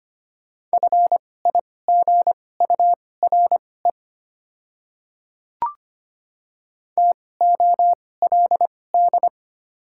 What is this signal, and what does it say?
Morse code, heard as a keyed tone, about 25 wpm, spaced as for 18 wpm, 700 Hz. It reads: FIGURE TOLD
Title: 2x Character Spacing - Top 500 Words - 25wpm